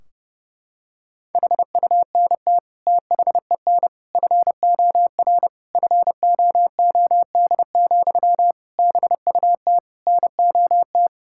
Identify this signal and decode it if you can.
Morse code, heard as a keyed tone, about 30 wpm, 700 Hz. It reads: HUNT THEN FOR FOOD, BUT NOT
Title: Sentences from Top 1600 Words 30wpm